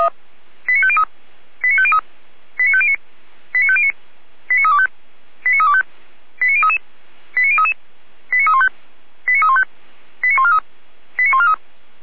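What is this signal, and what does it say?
Five-tone selective call (ZVEI), heard as quick runs of five tones, tones of about 70 ms. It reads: 89503 89503 89509 89509 89325 89325 89033 89033 89315 89315 89143 89143
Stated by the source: ZVEI five-tone selective call 89340 received on SDR.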